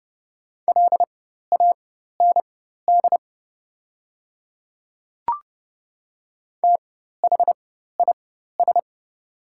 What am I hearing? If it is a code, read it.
Morse code, heard as a keyed tone, about 30 wpm, spaced as for 14 wpm, 700 Hz. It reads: LAND THIS
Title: 4x Character Spacing - Top 200 Words - 30wpm